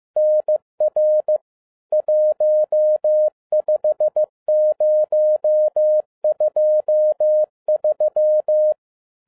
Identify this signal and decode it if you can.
Morse code, heard as a keyed tone, about 15 wpm, 610 Hz. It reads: NR 15023
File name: 2012-01-mp3 / n13-A NEWLY DESIGNED QSL CARD-15wpm